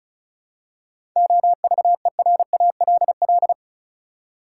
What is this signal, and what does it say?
Morse code, heard as a keyed tone, about 34 wpm, 700 Hz. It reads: OVERALL
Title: Seven Letter Words - 35wpm